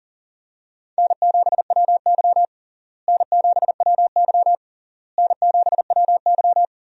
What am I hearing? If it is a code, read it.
Morse code, heard as a keyed tone, about 40 wpm, 700 Hz. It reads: N7WY N7WY N7WY
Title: USA Call Signs - 4 Characters - Repeated 3x - 40wpm